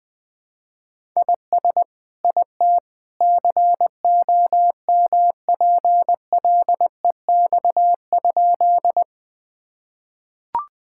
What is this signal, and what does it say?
Morse code, heard as a keyed tone, about 20 wpm, 700 Hz. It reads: IS IT COMPLEX?
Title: Sentences from Top 1000 Words - Review Part 6 - 20wpm